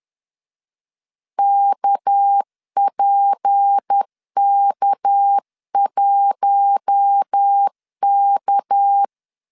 Morse code, heard as a keyed tone, about 10 wpm, 800 Hz. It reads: KPK1K
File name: LEZIONI / 6 PKX0 / 6 PK 10wpm 2436